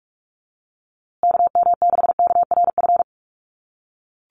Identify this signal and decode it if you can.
Morse code, heard as a keyed tone, about 44 wpm, 700 Hz. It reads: KM6KRF
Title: USA Call Signs Repeated 3x 45wpm